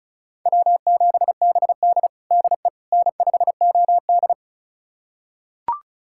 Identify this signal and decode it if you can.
Morse code, heard as a keyed tone, about 36 wpm, 700 Hz. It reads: W7BD DE N5OD
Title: USA Callsign DE Callsign - 4 Characters - 35wpm